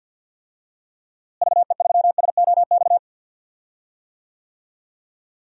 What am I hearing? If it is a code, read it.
Morse code, heard as a keyed tone, about 46 wpm, 700 Hz. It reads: VE3SOX
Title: DXCC-100 Call Signs - 6 Characters - Repeated 3x - 50wpm